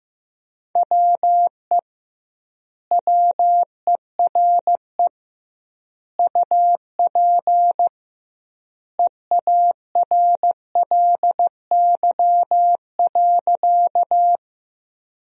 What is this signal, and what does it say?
Morse code, heard as a keyed tone, about 15 wpm, 700 Hz. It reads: WE WERE UP EARLY.
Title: 2x Word-Spacing - Sentences from Top 400 Words - 15wpm